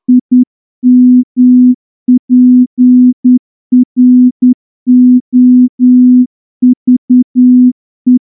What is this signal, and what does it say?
Morse code, heard as a keyed tone, about 9 wpm, 250 Hz. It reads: IMPROVE
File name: morse-training-10WPM